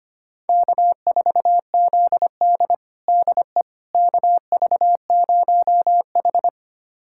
Morse code, heard as a keyed tone, about 25 wpm, 700 Hz. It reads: K4ZD DE KV0H
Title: USA Callsign DE Callsign - 4 Characters - 25wpm